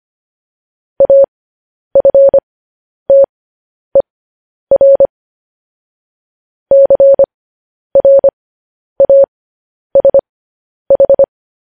Morse code, heard as a keyed tone, about 25 wpm, spaced as for 10 wpm, 550 Hz. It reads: AFTER CRASH
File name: podcast / News headlines-1x-25x10